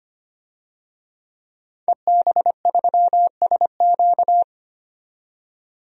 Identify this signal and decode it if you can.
Morse code, heard as a keyed tone, about 25 wpm, 700 Hz. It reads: EB3SQ
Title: DXCC-100 Call Signs - 5 Characters - 25wpm